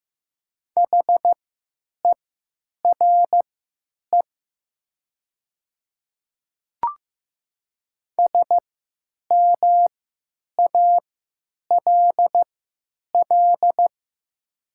Morse code, heard as a keyed tone, about 15 wpm, spaced as for 9 wpm, 700 Hz. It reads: HERE SMALL
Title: 3x Character Spacing - Top 200 Words - 15wpm